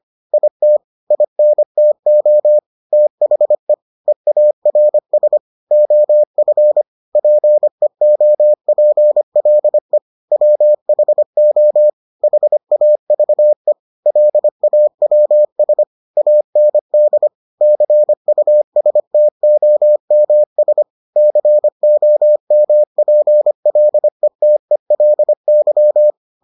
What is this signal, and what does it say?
Morse code, heard as a keyed tone, about 25 wpm, 600 Hz. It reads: IT INTO THE EARS OF PEOPLE WHO HAVE LAWS AND CUSTOMS COMPLETELY